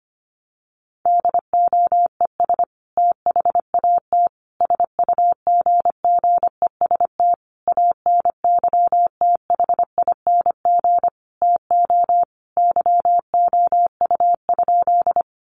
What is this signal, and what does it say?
Morse code, heard as a keyed tone, about 25 wpm, 700 Hz. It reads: DOES THAT SUGGEST ANYTHING TO YOU?
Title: Sentences from Top 1300 Words 25wpm